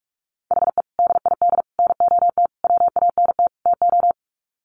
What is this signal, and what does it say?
Morse code, heard as a keyed tone, about 44 wpm, 700 Hz. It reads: HE DID NOT WANT TO
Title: Sets of 5 Words 45wpm